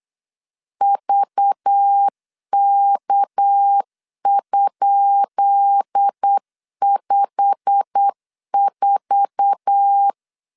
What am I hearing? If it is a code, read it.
Morse code, heard as a keyed tone, about 8 wpm, 800 Hz. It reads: VK?54
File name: LEZIONI / 10 SIGNS / 10B INT 8wpm 3534